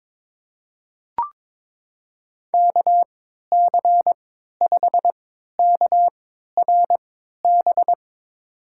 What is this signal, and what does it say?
Morse code, heard as a keyed tone, about 22 wpm, spaced as for 13 wpm, 700 Hz. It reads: KC5KRB
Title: USA Call Signs - 6 Characters - 3x Character Spacing - 22wpm